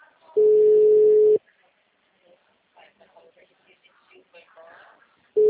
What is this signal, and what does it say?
Ringback tone (Europe, 425 Hz), with speech.